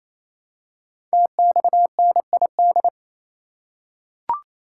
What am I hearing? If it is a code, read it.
Morse code, heard as a keyed tone, about 28 wpm, 700 Hz. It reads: TXNID
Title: Code Groups - 5 Letter:Numbers - 28wpm